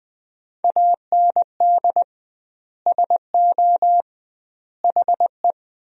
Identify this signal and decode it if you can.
Morse code, heard as a keyed tone, about 20 wpm, 700 Hz. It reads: AND SO HE
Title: 2x Word-Spacing - Sentences from Top 500 Words - 20wpm